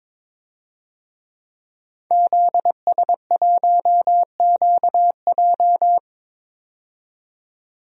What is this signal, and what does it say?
Morse code, heard as a keyed tone, about 22 wpm, 700 Hz. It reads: ZS1QJ
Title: DXCC-100 Call Signs - 5 Characters - 22wpm